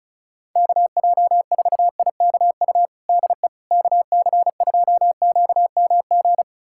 Morse code, heard as a keyed tone, about 36 wpm, 700 Hz. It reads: KJ4IKU DE KC2QMG
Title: USA Callsign DE Callsign - 6 Characters - 35wpm